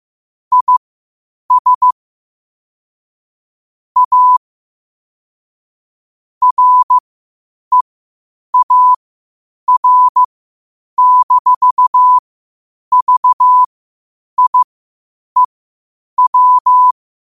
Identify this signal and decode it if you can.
Morse code, heard as a keyed tone, about 15 wpm, spaced as for 8 wpm, 1000 Hz. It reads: IS A REAR-VIEW
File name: qotdfiles / 8wpm